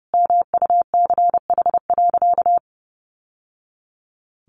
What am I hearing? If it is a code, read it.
Morse code, heard as a keyed tone, about 30 wpm, 700 Hz. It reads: MUCH.
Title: Sentences from Top 1200 Words 30wpm